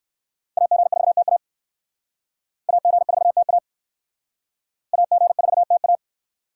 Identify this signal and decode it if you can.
Morse code, heard as a keyed tone, about 57 wpm, 700 Hz. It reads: AG4TA AG4TA AG4TA